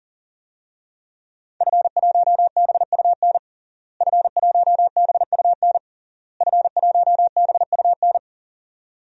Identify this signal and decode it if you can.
Morse code, heard as a keyed tone, about 40 wpm, 700 Hz. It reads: F1BUN F1BUN F1BUN
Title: DXCC-100 Call Signs - 5 Characters - Repeated 3x - 40wpm